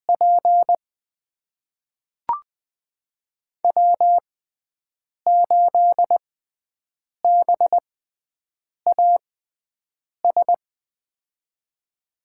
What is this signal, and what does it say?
Morse code, heard as a keyed tone, about 20 wpm, spaced as for 7 wpm, 700 Hz. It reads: P W8BAS
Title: USA Call Signs - 5 Characters - 6x Character Spacing - 20wpm